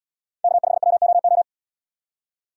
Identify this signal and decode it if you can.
Morse code, heard as a keyed tone, about 74 wpm, 700 Hz. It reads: K5WOO